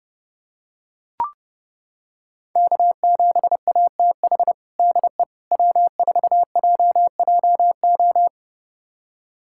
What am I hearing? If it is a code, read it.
Morse code, heard as a keyed tone, about 32 wpm, 700 Hz. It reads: K7ATH DE W4JJO
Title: USA Callsign DE Callsign - 5 Characters - 30wpm